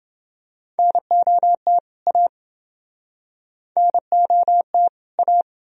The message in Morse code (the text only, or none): NOT A NOT A